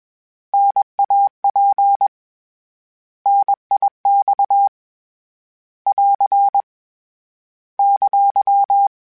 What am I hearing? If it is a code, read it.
Morse code, heard as a keyed tone, about 21 wpm, 800 Hz. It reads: NAP NIX <AR> !
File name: Contrasts / Contrasts-mp3s / 3LtrContrasts @20